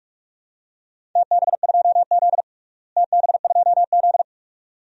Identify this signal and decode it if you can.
Morse code, heard as a keyed tone, about 42 wpm, 700 Hz. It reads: TB2Z TB2Z
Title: DXCC-100 Call Signs - 4 Characters - Repeated 2x - 45wpm